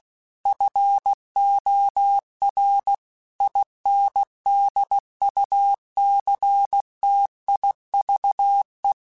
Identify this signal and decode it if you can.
Morse code, heard as a keyed tone, about 16 wpm, 800 Hz. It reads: FOR INDUCTIVE
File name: Technical Subjects / Technical Subjects-mp3s / XL @15